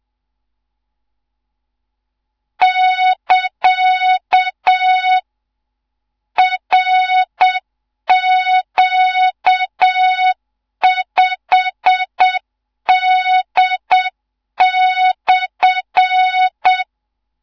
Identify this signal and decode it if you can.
Morse code, heard as a keyed tone, about 7 wpm, 3700 Hz. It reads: <KA> RQ5D/
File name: Morze-HA2MN / hang / 15 GYAKORLAT